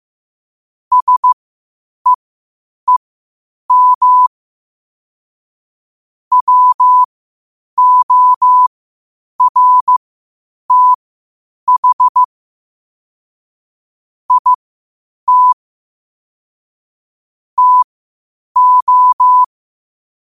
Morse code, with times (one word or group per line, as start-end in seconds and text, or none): 0.9-4.3 SEEM
6.3-12.2 WORTH
14.3-15.5 IT
17.6-19.4 TO